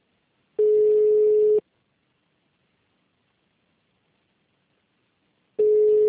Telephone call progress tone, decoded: ringback tone

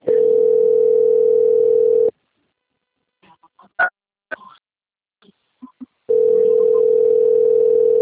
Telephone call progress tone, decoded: ringback tone

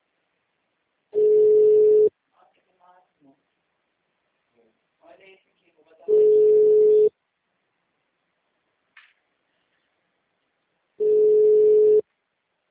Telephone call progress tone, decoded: ringback tone

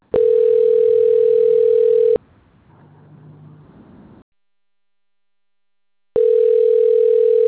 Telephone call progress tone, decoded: ringback tone

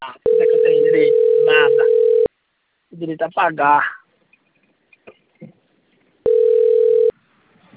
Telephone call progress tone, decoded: ringback tone

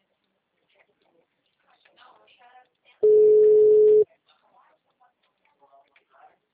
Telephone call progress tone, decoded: ringback tone

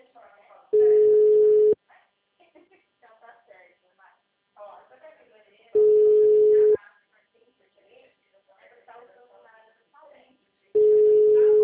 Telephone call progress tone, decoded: ringback tone

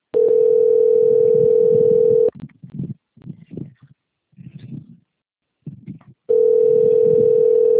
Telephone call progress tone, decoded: ringback tone

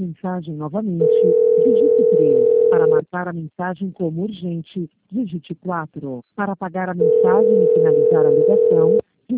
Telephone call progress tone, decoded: ringback tone

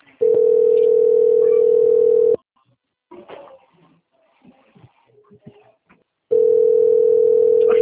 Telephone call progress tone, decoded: ringback tone

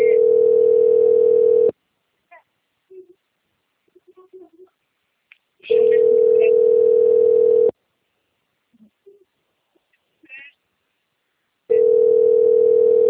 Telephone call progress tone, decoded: ringback tone

